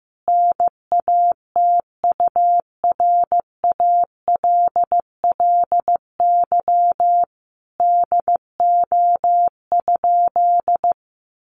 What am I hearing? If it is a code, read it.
Morse code, heard as a keyed tone, about 14 wpm, 700 Hz. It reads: NATURALLY DO?
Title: Sentences from Top 1300 Words 15wpm